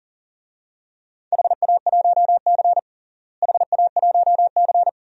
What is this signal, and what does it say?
Morse code, heard as a keyed tone, about 37 wpm, 700 Hz. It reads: HA1C HA1C
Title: DXCC-100 Call Signs - 4 Characters - Repeated 3x - 40wpm